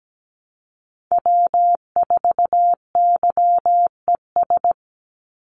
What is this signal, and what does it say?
Morse code, heard as a keyed tone, about 17 wpm, 700 Hz. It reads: W4YES
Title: Call Signs Repeated 3x - 17wpm